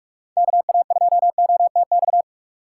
Morse code, heard as a keyed tone, about 47 wpm, 700 Hz. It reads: KA2OTX
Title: USA Call Signs - 2x Character Spacing - 45wpm